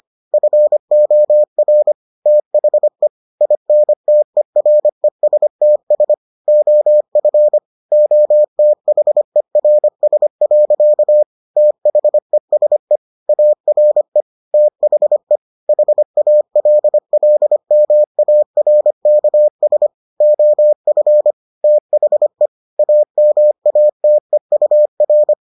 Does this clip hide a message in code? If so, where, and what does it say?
Yes: Morse code from 0.3 to 25.3 s, heard as a keyed tone, about 25 wpm, 600 Hz. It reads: FOR THE INTERESTS OF OTHERS. THESE ARE THE HALLMARKS OF THE AMATEUR